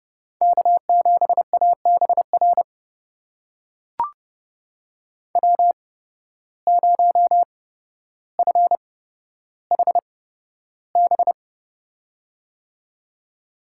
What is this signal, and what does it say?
Morse code, heard as a keyed tone, about 32 wpm, spaced as for 13 wpm, 700 Hz. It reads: K7ABR W0FHB
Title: USA Call Signs - 5 Characters - 8x Character Spacing - 30wpm